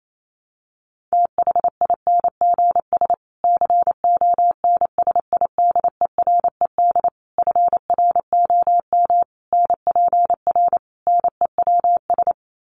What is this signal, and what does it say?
Morse code, heard as a keyed tone, about 28 wpm, 700 Hz. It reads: THINGS CONSIDERED FROM NPR NEWS
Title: Sets of 5 Words Encore 28wpm